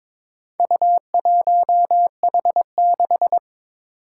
Morse code, heard as a keyed tone, about 22 wpm, 700 Hz. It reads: U1H6
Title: Four Challenging Letter-Number Combinations 22wpm